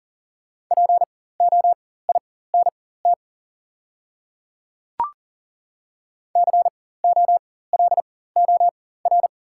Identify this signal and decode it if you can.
Morse code, heard as a keyed tone, about 40 wpm, spaced as for 19 wpm, 700 Hz. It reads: POINT COLOR
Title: 4x Character Spacing - Top 500 Words - 40wpm